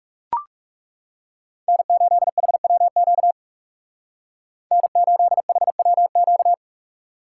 Morse code, heard as a keyed tone, about 39 wpm, 700 Hz. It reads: N8HWQ N8HWQ